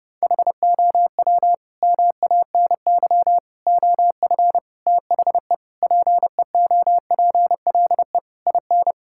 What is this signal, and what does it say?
Morse code, heard as a keyed tone, about 32 wpm, 700 Hz. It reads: HOW MANY OF THE PEOPLE IN
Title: Sentences from Top 1000 Words - Review Part 6 - 30wpm